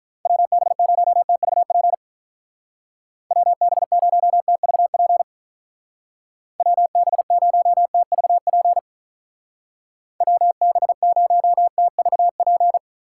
Morse code, heard as a keyed tone, about 43 wpm, 700 Hz. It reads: WB0TVP WB0TVP WB0TVP WB0TVP
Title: Speed Racing - USA 2x3 Call Signs 35wpm